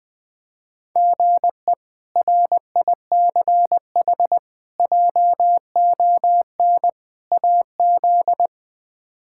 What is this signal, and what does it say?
Morse code, heard as a keyed tone, about 21 wpm, 700 Hz. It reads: GE RICH JON AZ
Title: K1USN SST QSO Practice - 20wpm